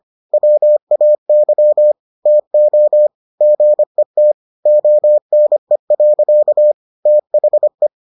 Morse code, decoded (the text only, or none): WAY TO GET ONE. THE